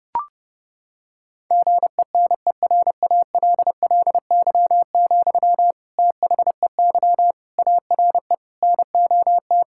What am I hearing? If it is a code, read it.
Morse code, heard as a keyed tone, about 30 wpm, 700 Hz. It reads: GENERALLY, THEY ARE NOT
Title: Sentences from Top 1000 Words - Review Part 6 - 30wpm